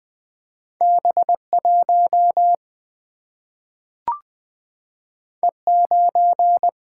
Morse code, heard as a keyed tone, about 20 wpm, 700 Hz. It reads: B 1 E 9
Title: Single-Letter Single-Number 20wpm